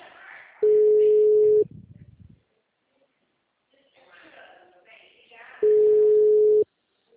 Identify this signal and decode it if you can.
Ringback tone (Europe, 425 Hz), with speech.
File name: out- -2039- - - .42329